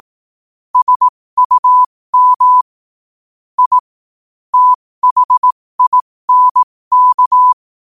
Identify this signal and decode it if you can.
Morse code, heard as a keyed tone, about 18 wpm, spaced as for 15 wpm, 1000 Hz. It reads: SUM I THINK
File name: qotdfiles / 15wpm